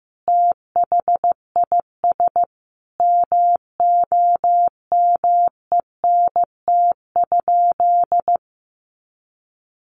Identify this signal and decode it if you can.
Morse code, heard as a keyed tone, about 15 wpm, 700 Hz. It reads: THIS MOMENT?
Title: Sentences from Top 1000 Words 15wpm